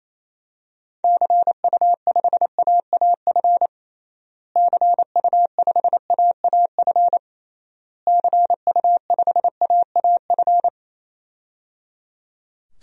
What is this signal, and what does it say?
Morse code, heard as a keyed tone, about 28 wpm, 700 Hz. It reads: CU5AAF CU5AAF CU5AAF
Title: DXCC-100 Call Signs - 6 Characters - Repeated 3x - 28wpm